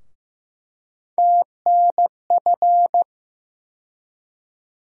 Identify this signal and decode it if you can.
Morse code, heard as a keyed tone, about 15 wpm, 700 Hz. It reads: TNF
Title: USA Call Sign Postfixes - 3 Letters 15wpm